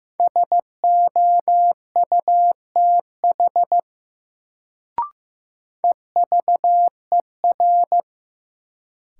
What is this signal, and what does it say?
Morse code, heard as a keyed tone, about 15 wpm, 700 Hz. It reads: SOUTH EVER
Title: Sentences from Top 400 Words - New Word Drill - 15wpm